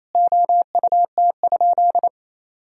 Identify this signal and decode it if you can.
Morse code, heard as a keyed tone, about 28 wpm, 700 Hz. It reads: OUT?
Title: Sentences from Top 200 Words and Abbreviations 28wpm